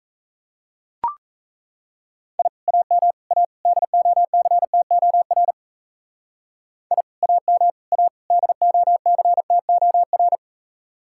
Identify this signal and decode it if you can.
Morse code, heard as a keyed tone, about 34 wpm, 700 Hz. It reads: I AM A DOCTOR I AM A DOCTOR
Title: Speed-Racing Common QSO Phrases - 28wpm